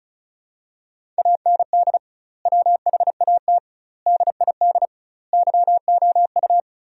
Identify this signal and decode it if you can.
Morse code, heard as a keyed tone, about 36 wpm, 700 Hz. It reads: AND WHAT DID YOU
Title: 2x Word-Spacing - Sentences from Top 100 Words - 35wpm